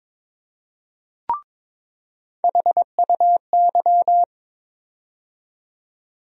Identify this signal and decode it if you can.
Morse code, heard as a keyed tone, about 23 wpm, 700 Hz. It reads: HUY
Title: USA Call Sign Postfixes - 3 Letters 22wpm